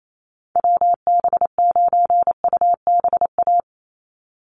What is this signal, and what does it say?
Morse code, heard as a keyed tone, about 28 wpm, 700 Hz. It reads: WB9UBA
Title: USA Call Signs 28wpm